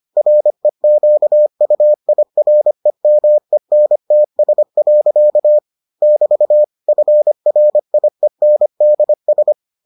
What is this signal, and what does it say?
Morse code, heard as a keyed tone, about 25 wpm, 600 Hz. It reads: REQUIREMENTS. <BT> FRIENDS